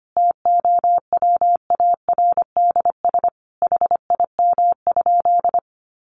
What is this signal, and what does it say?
Morse code, heard as a keyed tone, about 25 wpm, 700 Hz. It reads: TOWARDS HIM?
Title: Sentences from Top 1100 Words 25wpm